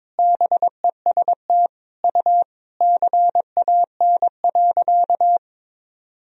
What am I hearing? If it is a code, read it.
Morse code, heard as a keyed tone, about 22 wpm, 700 Hz. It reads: BEST U CAN.
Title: Sentences from Top 500 Words and Abbreviations 22wpm